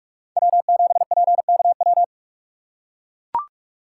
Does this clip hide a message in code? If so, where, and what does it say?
Yes: Morse code from 0.4 to 2.0 s, heard as a keyed tone, about 44 wpm, 700 Hz. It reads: W7PKW